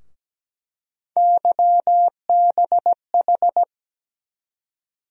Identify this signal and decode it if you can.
Morse code, heard as a keyed tone, about 17 wpm, 700 Hz. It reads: YBH
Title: USA Call Sign Postfixes - 3 Letters 17wpm